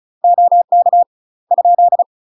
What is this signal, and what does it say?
Morse code, heard as a keyed tone, about 37 wpm, 700 Hz. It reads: OK ?
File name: UK MASTER QSOs TRAINING / 35WPM / UK MASTER QSOs 0040